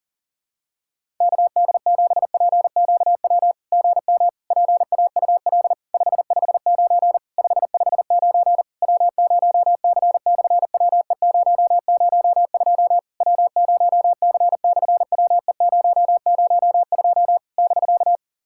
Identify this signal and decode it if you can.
Morse code, heard as a keyed tone, about 42 wpm, 700 Hz. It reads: KD7PQW GM PAUL 559 559 W0C/WE002 W0C/WE002 <BK>
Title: SOTA QSO Practice 40wpm